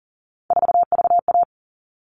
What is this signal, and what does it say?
Morse code, heard as a keyed tone, about 40 wpm, 700 Hz. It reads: 4VA
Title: All Cut Numbers - 3-Digit 40wpm